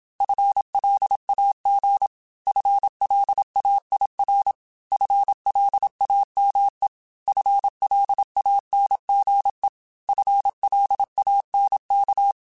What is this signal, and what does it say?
Morse code, heard as a keyed tone, about 27 wpm, 800 Hz. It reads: FLAG FLAIR FLAME FLANGE FLANK